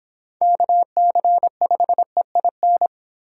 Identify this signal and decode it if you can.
Morse code, heard as a keyed tone, about 26 wpm, 700 Hz. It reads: KC5EIN